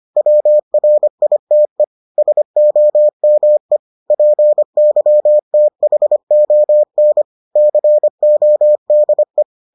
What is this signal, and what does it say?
Morse code, heard as a keyed tone, about 25 wpm, 600 Hz. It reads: WRITE SOME PYTHON CODE